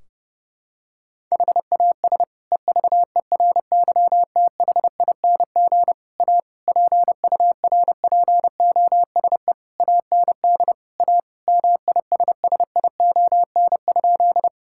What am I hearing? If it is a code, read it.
Morse code, heard as a keyed tone, about 30 wpm, 700 Hz. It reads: HAS EVERYTHING A PURPOSE AND A MISSION?
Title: Sentences from Top 1600 Words 30wpm